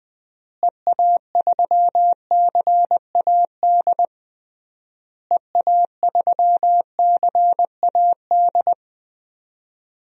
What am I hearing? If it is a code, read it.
Morse code, heard as a keyed tone, about 20 wpm, 700 Hz. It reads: EA3CAD EA3CAD
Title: DXCC-100 Call Signs - 6 Characters - Repeated 3x - 20wpm